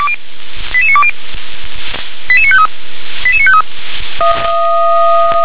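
Five-tone selective call (ZVEI), heard as quick runs of five tones, tones of about 70 ms. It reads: 88920 88953 88953
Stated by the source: ZVEI five-tone selective call 88952, received by SDR.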